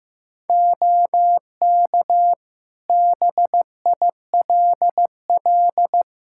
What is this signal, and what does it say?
Morse code, heard as a keyed tone, about 15 wpm, 700 Hz. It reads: OK BILL
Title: Practice QSO - Exchange 4 - Ending - 15wpm